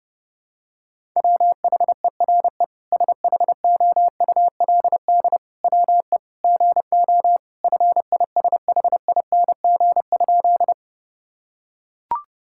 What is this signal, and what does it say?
Morse code, heard as a keyed tone, about 30 wpm, 700 Hz. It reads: WHERE SHOULD WE GO FISHING?
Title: Sentences from Top 1000 Words - Review Part 5 - 30wpm